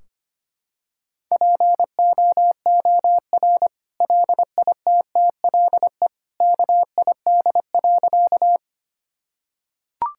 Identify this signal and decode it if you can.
Morse code, heard as a keyed tone, about 25 wpm, 700 Hz. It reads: POOR LITTLE KID.
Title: Sentences from Top 1600 Words 25wpm